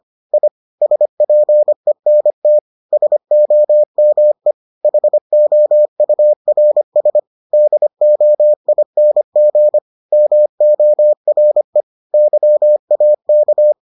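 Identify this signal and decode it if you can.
Morse code, heard as a keyed tone, about 25 wpm, 600 Hz. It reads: I SPENT SOME HOURS DOING MORE YAK
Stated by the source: Morse edition of the Foundations of Amateur Radio episodes.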